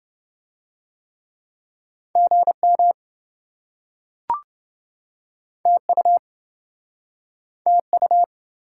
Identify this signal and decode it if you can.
Morse code, heard as a keyed tone, about 27 wpm, 700 Hz. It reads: GM TU TU